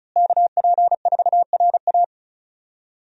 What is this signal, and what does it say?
Morse code, heard as a keyed tone, about 34 wpm, 700 Hz. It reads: KP4RA
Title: DXCC-100 Call Signs - 5 Characters - Repeated 3x - 35wpm